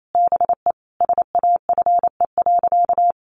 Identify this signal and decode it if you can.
Morse code, heard as a keyed tone, about 28 wpm, 700 Hz. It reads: BE SAFE.
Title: Sentences from Top 1300 Words 28wpm